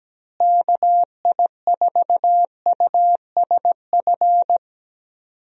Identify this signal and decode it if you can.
Morse code, heard as a keyed tone, about 17 wpm, 700 Hz. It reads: KI4USF
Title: USA Call Signs - 6 Characters - Repeated 2x - 17wpm